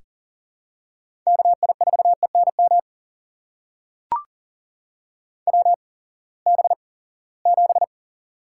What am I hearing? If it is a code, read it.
Morse code, heard as a keyed tone, about 38 wpm, spaced as for 22 wpm, 700 Hz. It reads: KI4ENM WB7
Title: USA Call Signs - 6 Characters - 8x Character Spacing - 40wpm